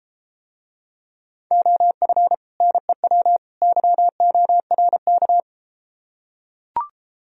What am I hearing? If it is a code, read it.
Morse code, heard as a keyed tone, about 33 wpm, 700 Hz. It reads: OF NEW YORK